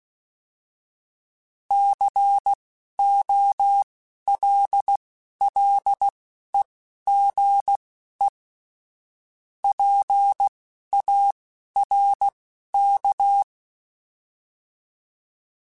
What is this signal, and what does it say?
Morse code, heard as a keyed tone, about 16 wpm, spaced as for 11 wpm, 800 Hz. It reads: COLLEGE PARK